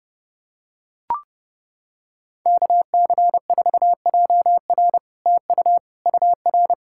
Morse code, heard as a keyed tone, about 29 wpm, 700 Hz. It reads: KC4JR TU UR